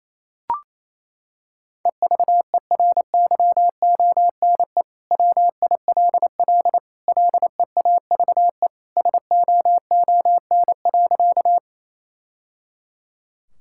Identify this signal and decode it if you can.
Morse code, heard as a keyed tone, about 28 wpm, 700 Hz. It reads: EVERYONE WILL LEAVE SOON.
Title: Sentences from Top 1000 Words - Review Part 5 - 28wpm